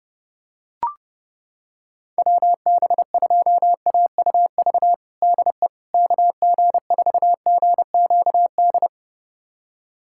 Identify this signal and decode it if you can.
Morse code, heard as a keyed tone, about 30 wpm, 700 Hz. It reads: WB2AUV DE KG4GQD
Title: USA Callsign DE Callsign - 6 Characters - 30wpm